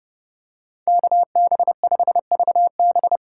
Morse code, heard as a keyed tone, about 29 wpm, 700 Hz. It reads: KB5VB